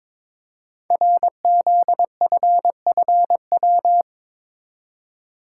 Morse code, heard as a keyed tone, about 22 wpm, 700 Hz. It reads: RZFFW
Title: Code Groups - 5 Letters - 22wpm